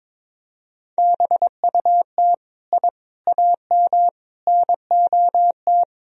Morse code, heard as a keyed tone, about 21 wpm, 700 Hz. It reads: BUT I AM NOT
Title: Sentences from Top 1000 Words - Review Part 2 - 22wpm